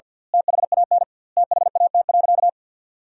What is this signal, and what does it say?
Morse code, heard as a keyed tone, about 50 wpm, 700 Hz. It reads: THAN THAT.
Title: 2x Word-Spacing - Sentences from Top 400 Words - 50wpm